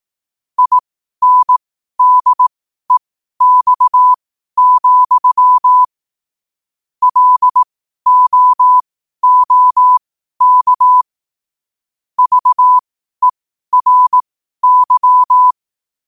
Morse code, heard as a keyed tone, about 18 wpm, spaced as for 12 wpm, 1000 Hz. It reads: INDEX, LOOK VERY
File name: qotdfiles / 12wpm